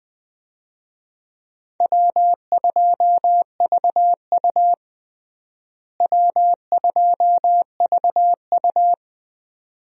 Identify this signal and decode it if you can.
Morse code, heard as a keyed tone, about 20 wpm, 700 Hz. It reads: W2VU W2VU
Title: USA Call Signs - 4 Characters - Repeated 2x - 20wpm